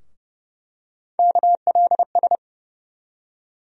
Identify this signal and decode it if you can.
Morse code, heard as a keyed tone, about 30 wpm, 700 Hz. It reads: KLS
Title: USA Call Sign Postfixes - 3 Letters 30wpm